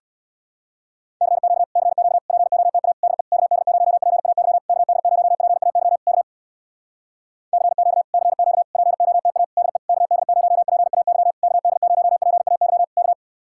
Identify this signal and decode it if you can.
Morse code, heard as a keyed tone, about 66 wpm, 700 Hz. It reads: CQ CQ POTA DE KD0PAY KD0PAY K CQ CQ POTA DE KD0PAY KD0PAY K